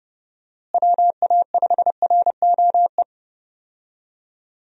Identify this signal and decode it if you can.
Morse code, heard as a keyed tone, about 32 wpm, 700 Hz. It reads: WA5ROE
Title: USA Call Signs - 6 Characters - Repeated 3x - 30wpm